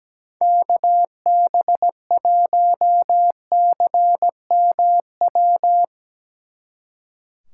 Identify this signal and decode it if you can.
Morse code, heard as a keyed tone, about 18 wpm, 700 Hz. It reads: KB1CMW
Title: Speed Racing - USA Call Signs 17wpm